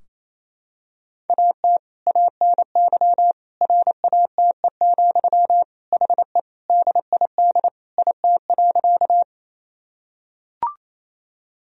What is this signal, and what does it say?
Morse code, heard as a keyed tone, about 28 wpm, 700 Hz. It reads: AT ANY RATE, HE DID IT.
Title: Sentences from Top 1000 Words - Review Part 4 - 28wpm